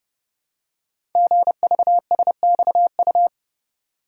Morse code, heard as a keyed tone, about 30 wpm, 700 Hz. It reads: GVSXU
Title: Code Groups - 5 Letters - 30wpm